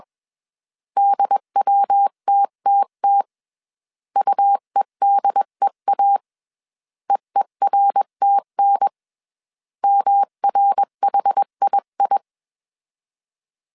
Morse code, heard as a keyed tone, about 21 wpm, 800 Hz. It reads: DWTTT UEDEA EERTN MRHII
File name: 20wpm